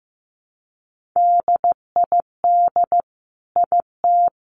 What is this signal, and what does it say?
Morse code, heard as a keyed tone, about 15 wpm, 700 Hz. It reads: DID IT
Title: Sentences from Top 1200 Words 15wpm